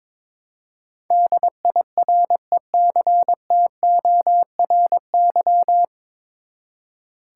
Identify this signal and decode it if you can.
Morse code, heard as a keyed tone, about 22 wpm, 700 Hz. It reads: DIRECTORY
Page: Morse Code Ninja practice 22wpm